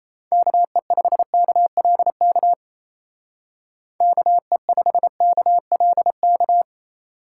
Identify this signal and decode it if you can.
Morse code, heard as a keyed tone, about 33 wpm, 700 Hz. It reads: KE5KLK KE5KLK
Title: Speed Racing - USA 2x3 Call Signs 28wpm